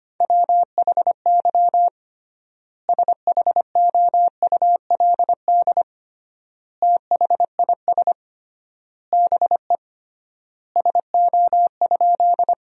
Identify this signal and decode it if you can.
Morse code, heard as a keyed tone, about 25 wpm, 700 Hz. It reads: WHY SHOULD THIS BE SO?
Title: 3x Word-Spacing - Sentences from Top 300 Words - 25wpm